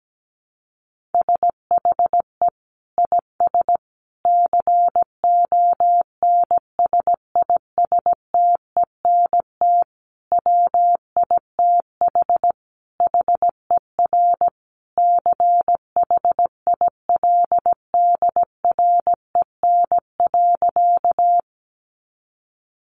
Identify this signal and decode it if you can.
Morse code, heard as a keyed tone, about 17 wpm, 700 Hz. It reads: SHE IS CONSISTENT WITH HER CHILDREN.
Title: Sentences from Top 1500 Words 17wpm